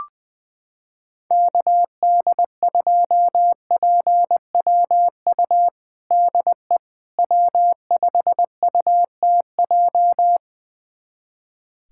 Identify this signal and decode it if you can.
Morse code, heard as a keyed tone, about 20 wpm, 700 Hz. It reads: KD2PWU DE W5UTJ
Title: USA Callsign DE Callsign 20wpm